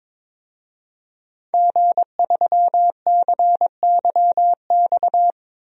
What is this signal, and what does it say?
Morse code, heard as a keyed tone, about 22 wpm, 700 Hz. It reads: G3CYX
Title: DXCC-100 Call Signs - 5 Characters - 22wpm